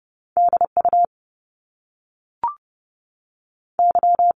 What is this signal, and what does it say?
Morse code, heard as a keyed tone, about 30 wpm, 700 Hz. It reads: D U Y
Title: Top Two Letter Combinations in Words 30wpm